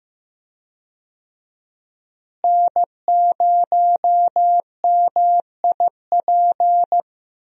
Morse code, heard as a keyed tone, about 16 wpm, 700 Hz. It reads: N0MIP
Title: USA Call Signs - 5 Characters - Repeated 2x - 15wpm